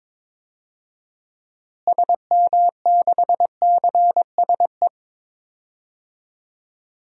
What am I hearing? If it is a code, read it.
Morse code, heard as a keyed tone, about 22 wpm, 700 Hz. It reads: SM6CSE
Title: DXCC-100 Call Signs - 6 Characters - Repeated 2x - 22wpm